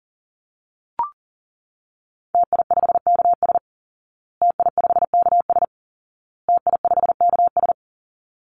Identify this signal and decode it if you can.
Morse code, heard as a keyed tone, about 37 wpm, 700 Hz. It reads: TI5KS TI5KS TI5KS
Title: Call Signs Repeated 2x - 40wpm